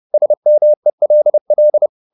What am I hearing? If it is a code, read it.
Morse code, heard as a keyed tone, about 30 wpm, 600 Hz. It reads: SMELL
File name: the old man and the sea / the old man and the sea 1 30wpm